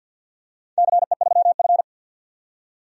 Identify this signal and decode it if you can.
Morse code, heard as a keyed tone, about 46 wpm, 700 Hz. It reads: CE3F
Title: DXCC-100 Call Signs - 4 Characters - 50wpm